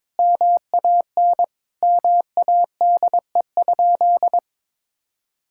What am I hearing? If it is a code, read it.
Morse code, heard as a keyed tone, about 22 wpm, 700 Hz. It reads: MAN MADE?
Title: Sentences from Top 1000 Words - Review Part 6 - 22wpm